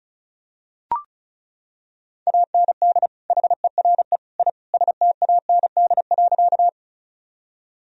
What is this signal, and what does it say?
Morse code, heard as a keyed tone, about 34 wpm, 700 Hz. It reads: AND HERE I STAND.
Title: Sentences from Top 1000 Words - Review Part 1 - 35wpm